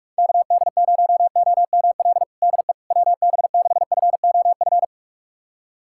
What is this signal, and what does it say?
Morse code, heard as a keyed tone, about 42 wpm, 700 Hz. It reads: KD0OML DE WB6FOF
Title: USA Callsign DE Callsign 45wpm